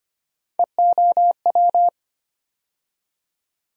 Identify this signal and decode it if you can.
Morse code, heard as a keyed tone, about 25 wpm, 700 Hz. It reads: EOW
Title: USA Call Sign Postfixes - 3 Letters 25wpm